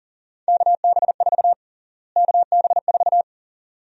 Morse code, heard as a keyed tone, about 40 wpm, 700 Hz. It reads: KB4 KB4
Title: USA Call Sign Prefixes - Repeated 2x - 40wpm